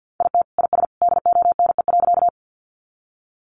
Morse code, heard as a keyed tone, about 50 wpm, 700 Hz. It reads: IT IS DONE.
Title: Sentences from Top 500 Words 50wpm